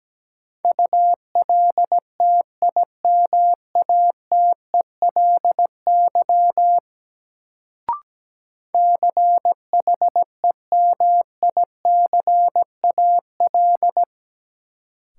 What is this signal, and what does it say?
Morse code, heard as a keyed tone, about 17 wpm, 700 Hz. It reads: ULTIMATELY CHEMICAL
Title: Sentences from Top 1200 Words - New Word Drill - 17wpm